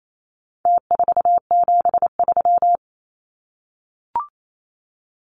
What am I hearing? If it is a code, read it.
Morse code, heard as a keyed tone, about 28 wpm, 700 Hz. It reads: T473
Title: Numbers - Contest Style - 4-Digit 28wpm